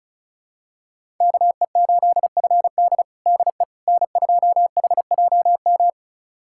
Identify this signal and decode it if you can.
Morse code, heard as a keyed tone, about 34 wpm, 700 Hz. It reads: KE8FD DE N2HJM
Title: USA Callsign DE Callsign - 5 Characters - 35wpm